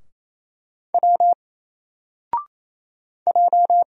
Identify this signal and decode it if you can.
Morse code, heard as a keyed tone, about 27 wpm, 700 Hz. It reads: W J
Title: Single Letter-Number Alt 28wpm